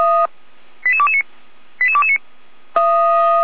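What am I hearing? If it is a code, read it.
Five-tone selective call (ZVEI), heard as quick runs of five tones, tones of about 70 ms. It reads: 88229 88229
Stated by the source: ZVEI five-tone selective call received by SDR, from Feuerwehr Orlamünde.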